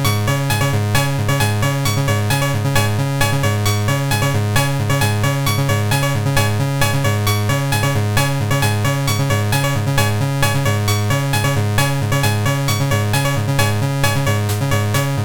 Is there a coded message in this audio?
No.